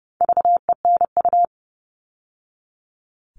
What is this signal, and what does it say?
Morse code, heard as a keyed tone, about 30 wpm, 700 Hz. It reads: VENU
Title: All Cut Numbers - 4-Digit 30wpm